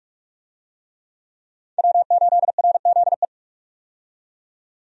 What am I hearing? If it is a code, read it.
Morse code, heard as a keyed tone, about 46 wpm, 700 Hz. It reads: W8RZE
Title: USA Call Signs - 5 Characters - 45wpm